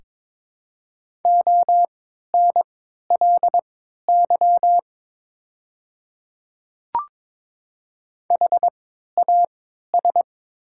Morse code, heard as a keyed tone, about 22 wpm, spaced as for 13 wpm, 700 Hz. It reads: ONLY HAS
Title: 3x Character Spacing - Top 200 Words - 22wpm